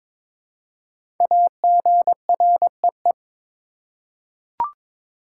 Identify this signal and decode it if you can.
Morse code, heard as a keyed tone, about 22 wpm, 700 Hz. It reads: AGREE
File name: 5 Letter Words - First 3 Letters Given - 22wpm